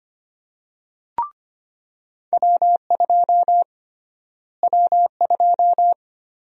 Morse code, heard as a keyed tone, about 25 wpm, 700 Hz. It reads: W2 W2
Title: USA Call Sign Prefixes - Repeated 2x - 25wpm